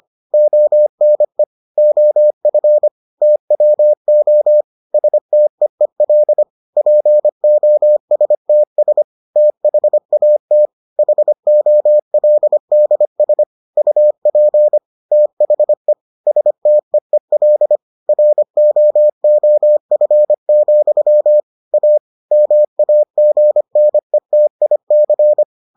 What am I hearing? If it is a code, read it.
Morse code, heard as a keyed tone, about 25 wpm, 600 Hz. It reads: ONE OF TWO STEEL POSTS THAT HOLDS UP THE STEEL ROOF, A MAGNETIC